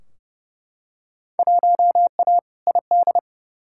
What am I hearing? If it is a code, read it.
Morse code, heard as a keyed tone, about 30 wpm, 700 Hz. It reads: 1A ID
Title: Field Day Exchanges - 30wpm